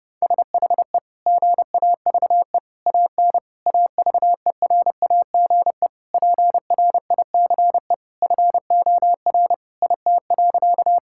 Morse code, heard as a keyed tone, about 31 wpm, 700 Hz. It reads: SHE GAVE AN AVERAGE PRICE FOR IT.
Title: Sentences from Top 1000 Words - Review Part 6 - 30wpm